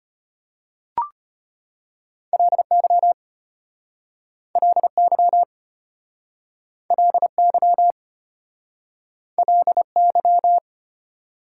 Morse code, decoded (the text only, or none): LY LY LY LY